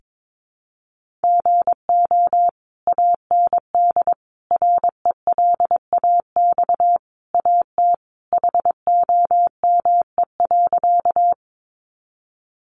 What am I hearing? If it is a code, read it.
Morse code, heard as a keyed tone, about 22 wpm, 700 Hz. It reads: GO AND RELAX AT HOME.
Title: Sentences from Top 1400 Words 22wpm